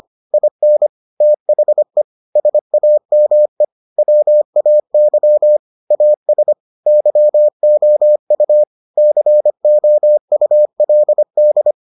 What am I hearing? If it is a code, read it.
Morse code, heard as a keyed tone, about 25 wpm, 600 Hz. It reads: IN THE SAME WAY AS YOU COULD